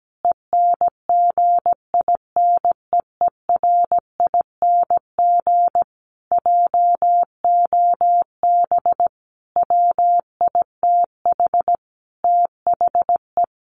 Morse code, heard as a keyed tone, about 17 wpm, 700 Hz. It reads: ENGINEERING JOB WITH THE
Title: Sentences from Top 1500 Words 17wpm